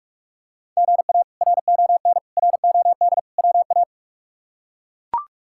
Morse code, heard as a keyed tone, about 41 wpm, 700 Hz. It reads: GA RON ROD WA
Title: K1USN SST QSO Practice - 45wpm